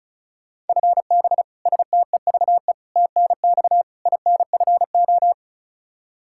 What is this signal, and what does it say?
Morse code, heard as a keyed tone, about 36 wpm, 700 Hz. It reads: FB STEVE TNX INFO
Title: Practice QSO - Exchange 3 - More Info - 35wpm